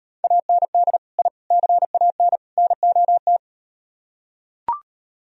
Morse code, heard as a keyed tone, about 38 wpm, 700 Hz. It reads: AND I CAN NOT